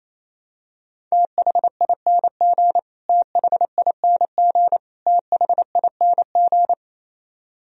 Morse code, heard as a keyed tone, about 27 wpm, 700 Hz. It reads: THING THING THING
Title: Top 100 Words Repeated 28wpm